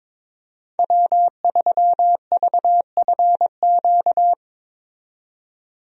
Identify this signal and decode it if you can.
Morse code, heard as a keyed tone, about 22 wpm, 700 Hz. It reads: W3VFQ
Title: USA Call Signs - 5 Characters - 22wpm